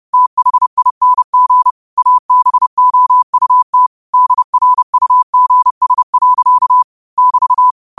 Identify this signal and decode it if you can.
Morse code, heard as a keyed tone, about 30 wpm, 1000 Hz. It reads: THING ABOUT DRUGS. <BT>
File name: qotdfiles / 30wpm